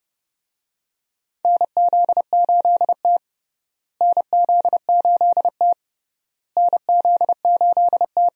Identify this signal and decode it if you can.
Morse code, heard as a keyed tone, about 30 wpm, 700 Hz. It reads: NZ8T NZ8T NZ8T
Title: USA Call Signs - 4 Characters - Repeated 3x - 30wpm